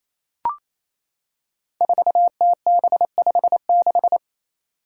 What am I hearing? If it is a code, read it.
Morse code, heard as a keyed tone, about 28 wpm, 700 Hz. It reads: 4TB56
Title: Five Challenging Letter-Number Combinations 28wpm